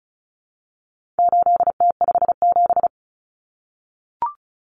Morse code, heard as a keyed tone, about 34 wpm, 700 Hz. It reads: 8T57
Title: Common Cut Numbers - 4-Digit 35wpm